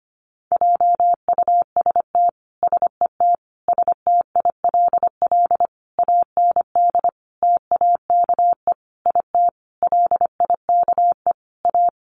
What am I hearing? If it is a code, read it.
Morse code, heard as a keyed tone, about 25 wpm, 700 Hz. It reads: JUST SET STILL AND TAKE IT LIKE A
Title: Sentences from Top 300 Words 25wpm